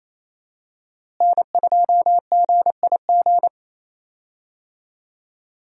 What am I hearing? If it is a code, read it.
Morse code, heard as a keyed tone, about 28 wpm, 700 Hz. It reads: N2GIG
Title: USA Call Signs - 5 Characters - Repeated 3x - 28wpm